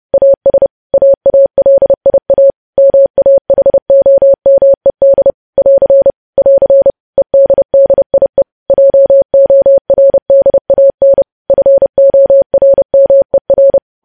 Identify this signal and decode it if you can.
Morse code, heard as a keyed tone, about 30 wpm, 550 Hz. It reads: AS AALIA MAHOMED <AR> <AR> EDDIE JORDAN FORMER
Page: news headlines in morse code at 30 wpm